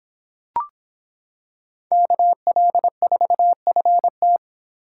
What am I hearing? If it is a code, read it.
Morse code, heard as a keyed tone, about 26 wpm, 700 Hz. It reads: KL4FT